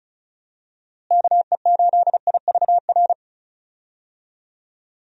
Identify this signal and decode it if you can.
Morse code, heard as a keyed tone, about 34 wpm, 700 Hz. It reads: KE8IVR
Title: USA Call Signs - 6 Characters - Repeated 2x - 35wpm